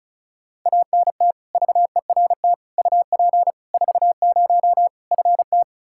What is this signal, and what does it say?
Morse code, heard as a keyed tone, about 34 wpm, 700 Hz. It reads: ANT VERT UP 40 FT
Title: Practice QSO - Exchange 2 - Info - 35wpm